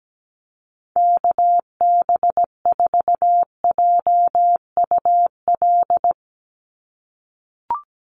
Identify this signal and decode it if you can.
Morse code, heard as a keyed tone, about 17 wpm, 700 Hz. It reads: KB4JUL
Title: USA Call Signs 17wpm